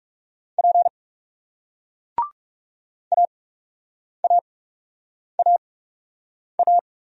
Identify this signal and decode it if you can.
Morse code, heard as a keyed tone, about 38 wpm, spaced as for 8 wpm, 700 Hz. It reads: P AAAA